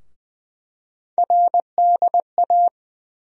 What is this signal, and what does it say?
Morse code, heard as a keyed tone, about 20 wpm, 700 Hz. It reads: RDA
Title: USA Call Sign Postfixes - 3 Letters 20wpm